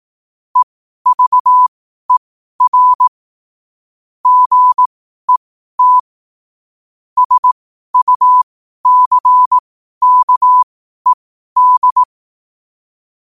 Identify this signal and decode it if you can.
Morse code, heard as a keyed tone, about 18 wpm, spaced as for 12 wpm, 1000 Hz. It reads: EVER GET SUCKED
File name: qotdfiles / 12wpm